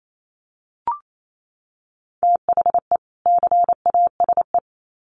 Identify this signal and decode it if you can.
Morse code, heard as a keyed tone, about 28 wpm, 700 Hz. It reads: THE CASE
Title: Sets of 2 Words 28wpm